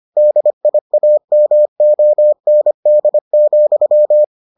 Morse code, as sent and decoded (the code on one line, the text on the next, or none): -.. .. .- -- --- -. -.. --..--
DIAMOND,